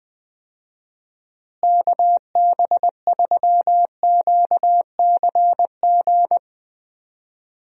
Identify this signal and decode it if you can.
Morse code, heard as a keyed tone, about 20 wpm, 700 Hz. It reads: KB3QCG
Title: USA Call Signs - 6 Characters - Repeated 3x - 20wpm